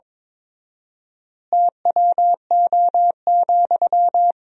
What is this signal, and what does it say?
Morse code, heard as a keyed tone, about 22 wpm, 700 Hz. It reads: TWO,
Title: 4x Word-Spacing - Sentences from Top 200 Words - 22wpm